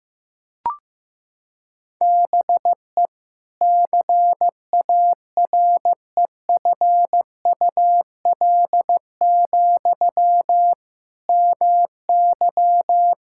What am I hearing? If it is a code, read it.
Morse code, heard as a keyed tone, about 15 wpm, 700 Hz. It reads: BE CAREFUL, MY
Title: Sentences from Top 1600 Words 15wpm